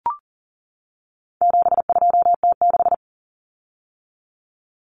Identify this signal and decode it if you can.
Morse code, heard as a keyed tone, about 40 wpm, 700 Hz. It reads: 72T6
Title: Common Cut Numbers - 4-Digit 40wpm